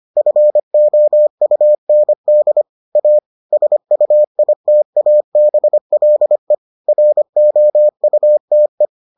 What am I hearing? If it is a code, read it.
Morse code, heard as a keyed tone, about 25 wpm, 600 Hz. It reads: FOUND A SUITABLE ROUTE